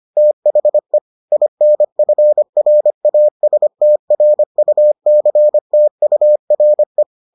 Morse code, heard as a keyed tone, about 25 wpm, 600 Hz. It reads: THE INFRASTRUCTURE